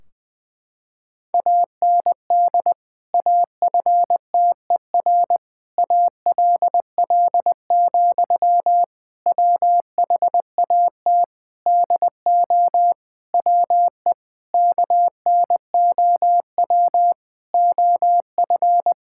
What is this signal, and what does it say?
Morse code, heard as a keyed tone, about 19 wpm, 700 Hz. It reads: AND AFTER ALL, WHAT DO WE KNOW OF
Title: Sentences from Top 1600 Words 20wpm